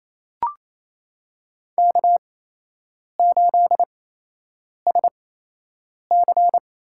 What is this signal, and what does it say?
Morse code, heard as a keyed tone, about 28 wpm, spaced as for 8 wpm, 700 Hz. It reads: K8SC
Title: USA Call Signs - 4 Characters - 8x Character Spacing - 28wpm